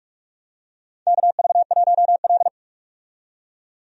Morse code, heard as a keyed tone, about 44 wpm, 700 Hz. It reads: KV1L
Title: USA Call Signs - 4 Characters - 45wpm